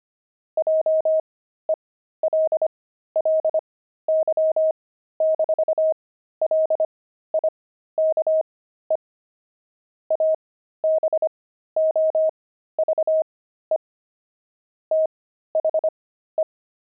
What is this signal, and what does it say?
Morse code, heard as a keyed tone, about 25 wpm, spaced as for 13 wpm, 630 Hz. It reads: JELLY-LIKE ABOVE THE
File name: pangram-fs-13-cs-25-jelly